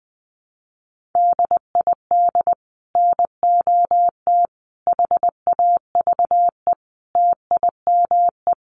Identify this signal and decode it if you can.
Morse code, heard as a keyed tone, about 20 wpm, 700 Hz. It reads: DID NOT HAVE TIME
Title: Sets of 5 Words Encore 20wpm